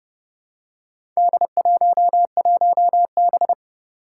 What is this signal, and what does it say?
Morse code, heard as a keyed tone, about 30 wpm, 690 Hz. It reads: D11B
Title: Four Challenging Letter-Number Combinations 30wpm